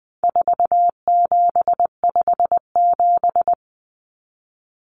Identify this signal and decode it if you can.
Morse code, heard as a keyed tone, about 20 wpm, 700 Hz. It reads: 4757
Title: Numbers - 4-Digit 20wpm